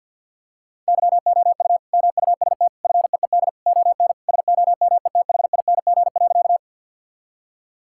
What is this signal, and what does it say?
Morse code, heard as a keyed tone, about 50 wpm, 700 Hz. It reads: YOU MUST FEED ON SOMETHING.
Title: Sentences from Top 1600 Words 50wpm